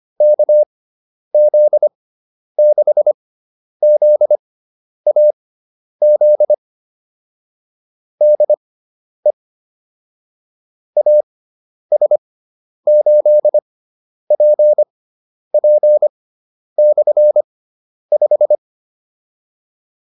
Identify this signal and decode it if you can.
Morse code, heard as a keyed tone, about 25 wpm, spaced as for 10 wpm, 600 Hz. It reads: KZ6ZAZ DE AS8PP/5
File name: QSO302 10